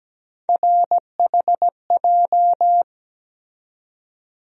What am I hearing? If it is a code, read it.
Morse code, heard as a keyed tone, about 17 wpm, 700 Hz. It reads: RHJ
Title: USA Call Sign Postfixes - 3 Letters 17wpm